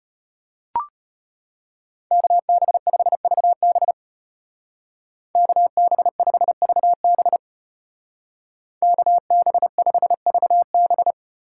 Morse code, heard as a keyed tone, about 32 wpm, 700 Hz. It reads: KB5VB KB5VB KB5VB